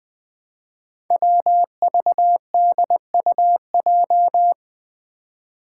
Morse code, heard as a keyed tone, about 20 wpm, 700 Hz. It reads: WVDUJ
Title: Five Challenging Letter-Number Combinations 20wpm